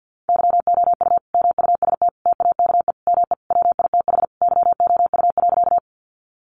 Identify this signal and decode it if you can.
Morse code, heard as a keyed tone, about 50 wpm, 700 Hz. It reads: YOU MUST TAKE ME WITH YOU.
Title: Sentences from Top 200 Words 50wpm